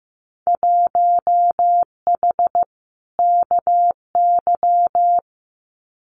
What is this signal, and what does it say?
Morse code, heard as a keyed tone, about 15 wpm, 700 Hz. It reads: 1H KY
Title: Winter Field Day Exchanges 15wpm